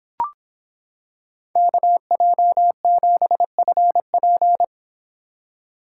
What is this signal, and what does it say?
Morse code, heard as a keyed tone, about 26 wpm, 700 Hz. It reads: KJ7FP